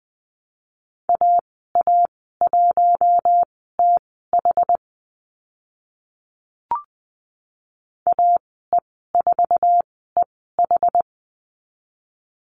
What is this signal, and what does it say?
Morse code, heard as a keyed tone, about 20 wpm, spaced as for 15 wpm, 700 Hz. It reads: AA1TH AE4EH
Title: 062 Lesson 11g - Letter h - Call Signs 20wpm